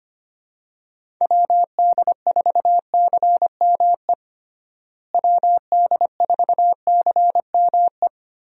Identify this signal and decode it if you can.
Morse code, heard as a keyed tone, about 25 wpm, 700 Hz. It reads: WD4CME WD4CME
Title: USA Call Signs - 6 Characters - Repeated 3x - 25wpm